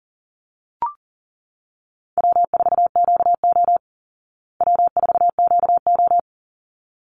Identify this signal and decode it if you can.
Morse code, heard as a keyed tone, about 40 wpm, 700 Hz. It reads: W4QO W4QO
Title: Easy Call Signs Repeated 2x - 40wpm